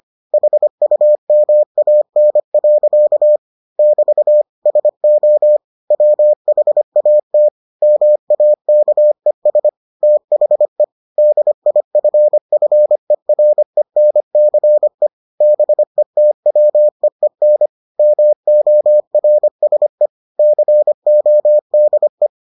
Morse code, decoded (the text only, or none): HUMAN. <BT> SO WHAT MAKES THE DIFFERENCE BETWEEN MORSE CODE